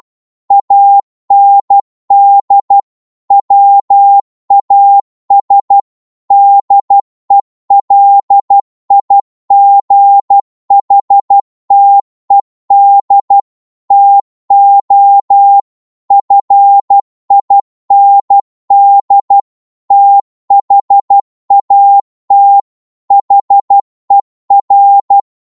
Morse code, heard as a keyed tone, about 12 wpm, 800 Hz. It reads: AND WAS DELIGHTED TO FIND THAT HER